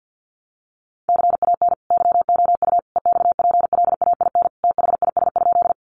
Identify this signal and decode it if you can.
Morse code, heard as a keyed tone, about 50 wpm, 700 Hz. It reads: CAN YOU EXPLAIN THIS?
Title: Sentences from Top 1100 Words 50wpm